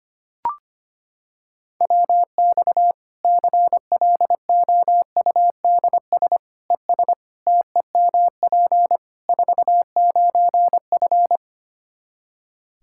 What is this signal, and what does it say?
Morse code, heard as a keyed tone, about 25 wpm, 700 Hz. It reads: WX CLOUDS ES TEMP 49F
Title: K1USN SST QSO Practice 25wpm